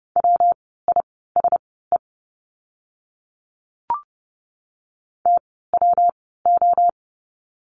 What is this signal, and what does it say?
Morse code, heard as a keyed tone, about 30 wpm, spaced as for 17 wpm, 700 Hz. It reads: WISE TWO
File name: course / 30:10wpm / 115 Lesson 19f - Letter w - Words 30wpm